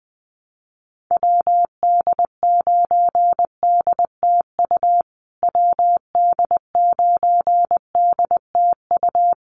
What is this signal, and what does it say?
Morse code, heard as a keyed tone, about 20 wpm, 700 Hz. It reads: WD9DTU WD9DTU
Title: USA Call Signs Repeated 3x 20wpm